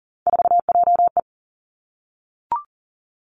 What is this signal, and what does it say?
Morse code, heard as a keyed tone, about 42 wpm, 700 Hz. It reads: 4JE
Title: Three Challenging Letter-Number Combinations 40wpm